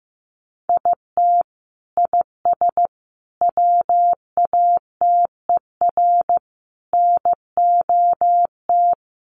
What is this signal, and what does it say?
Morse code, heard as a keyed tone, about 15 wpm, 700 Hz. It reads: IT IS WATER NOT
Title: Sentences from Top 1100 Words 15wpm